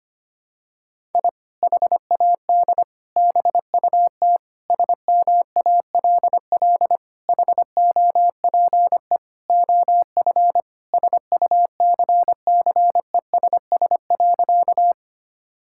Morse code, heard as a keyed tone, about 25 wpm, 700 Hz. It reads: I HAD BUT SMALL HOPE OF SUCCESS.
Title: Sentences from Top 1000 Words - Review Part 6 - 25wpm